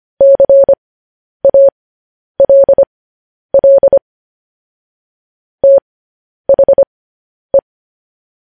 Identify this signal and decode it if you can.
Morse code, heard as a keyed tone, about 25 wpm, spaced as for 10 wpm, 550 Hz. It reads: CALL THE
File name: podcast / News headlines-1x-25x10